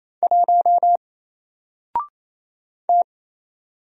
Morse code, heard as a keyed tone, about 27 wpm, 700 Hz. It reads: 1 T
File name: Single-Letter-Number-28wpm